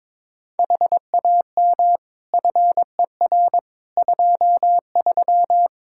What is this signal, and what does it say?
Morse code, heard as a keyed tone, about 22 wpm, 700 Hz. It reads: HAM FER 23
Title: Practice QSO - Exchange 3 - More Info - 22wpm